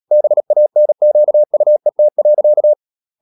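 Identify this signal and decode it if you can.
Morse code, heard as a keyed tone, about 37 wpm, 600 Hz. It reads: BANQUET.
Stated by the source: Morse practice at 36 wpm, text of Ken Follett's novel Never, from CW med Gnister.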